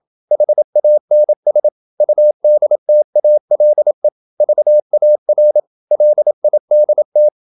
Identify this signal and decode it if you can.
Morse code, heard as a keyed tone, about 27 wpm, 600 Hz. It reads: HANS UDTALE VAR LIDT
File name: Ferie QRQ 27wpm